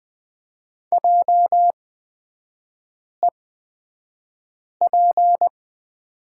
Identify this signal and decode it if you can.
Morse code, heard as a keyed tone, about 20 wpm, spaced as for 5 wpm, 700 Hz. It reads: JEP
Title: 236 Lesson 38e - Letter q - Three Characters 20wpm